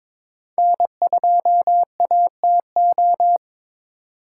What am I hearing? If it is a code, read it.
Morse code, heard as a keyed tone, about 26 wpm, 700 Hz. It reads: N2ATO
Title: Speed Racing - USA 1x3 Call Signs 22wpm